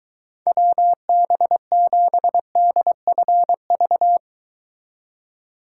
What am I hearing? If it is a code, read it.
Morse code, heard as a keyed tone, about 22 wpm, 700 Hz. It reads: WB7DFV